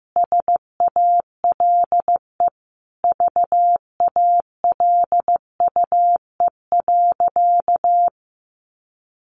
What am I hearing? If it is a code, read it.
Morse code, heard as a keyed tone, about 15 wpm, 700 Hz. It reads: SALE VALUE.
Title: Sentences from Top 1400 Words 15wpm